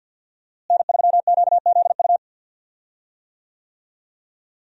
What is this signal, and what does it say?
Morse code, heard as a keyed tone, about 47 wpm, 700 Hz. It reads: N3QZU
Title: USA Call Signs - 5 Characters - Repeated 3x - 50wpm